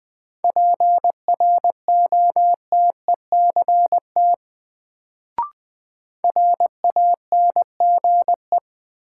Morse code, heard as a keyed tone, about 20 wpm, 700 Hz. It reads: PROTECT RANGE
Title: Sentences from Top 900 Words - New Word Drill - 20wpm